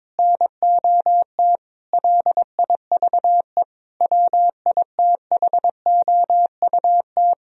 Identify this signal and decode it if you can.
Morse code, heard as a keyed tone, about 21 wpm, 700 Hz. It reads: NOT LIVE WITHOUT
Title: Sentences from Top 1000 Words - Review Part 3 - 22wpm